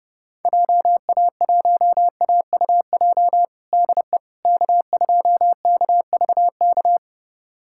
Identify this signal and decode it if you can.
Morse code, heard as a keyed tone, about 32 wpm, 700 Hz. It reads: JA1AUJ DE K2KVK
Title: USA Callsign DE Callsign 30wpm